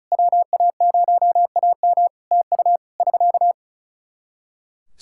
Morse code, heard as a keyed tone, about 36 wpm, 700 Hz. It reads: WA0AM TU <SK>
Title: Practice QSO - Exchange 4 - Ending - 35wpm